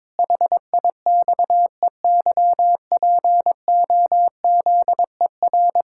Morse code, heard as a keyed tone, about 22 wpm, 700 Hz. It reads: HIXEYPOZER
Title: Pangram Challenge v2 - 22wpm